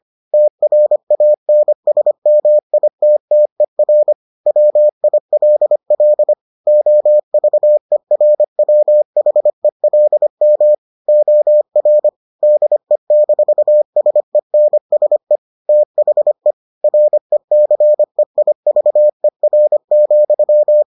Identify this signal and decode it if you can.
Morse code, heard as a keyed tone, about 25 wpm, 600 Hz. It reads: TRANSMITTER WILL OVERWHELM OR DE-SENSE THE RECEIVER,